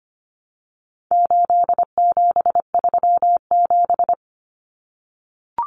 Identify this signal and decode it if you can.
Morse code, heard as a keyed tone, about 25 wpm, 700 Hz. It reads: 8737
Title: Common Cut Numbers - 4-Digit 25wpm